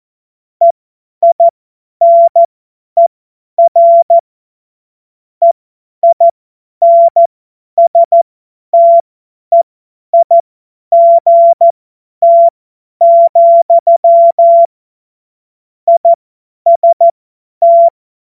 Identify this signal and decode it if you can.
Morse code, heard as a keyed tone, about 14 wpm, spaced as for 10 wpm, 680 Hz. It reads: EINER EINSTEIGT, IST
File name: Morsekurs / Witze / Witz 26. April 2022 50 F70